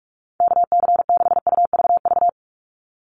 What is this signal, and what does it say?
Morse code, heard as a keyed tone, about 44 wpm, 700 Hz. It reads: KC6UVV
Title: USA Call Signs 45wpm